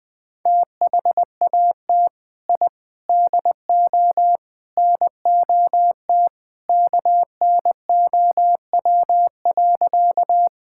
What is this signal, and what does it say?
Morse code, heard as a keyed tone, about 20 wpm, 700 Hz. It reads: THAT I DO NOT KNOW.
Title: Sentences with 4-Word Phrases - 20wpm